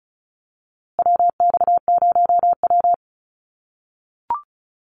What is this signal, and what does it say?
Morse code, heard as a keyed tone, about 34 wpm, 700 Hz. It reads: WX0W
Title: USA Call Signs Repeated 2x 35wpm